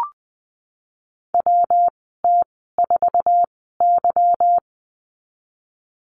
Morse code, heard as a keyed tone, about 20 wpm, 700 Hz. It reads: W T 4 Y
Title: 154 Lesson 25g - Letter p - Call Signs 20wpm